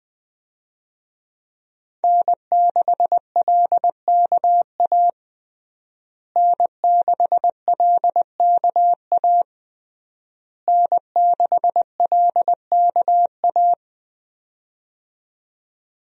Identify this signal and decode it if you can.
Morse code, heard as a keyed tone, about 20 wpm, 700 Hz. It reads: N6LKA N6LKA N6LKA
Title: USA Call Signs - 5 Characters - Repeated 3x - 20wpm